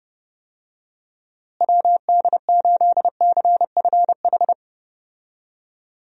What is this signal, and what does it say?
Morse code, heard as a keyed tone, about 32 wpm, 700 Hz. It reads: WD8CFH
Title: USA Call Signs - 6 Characters - Repeated 2x - 30wpm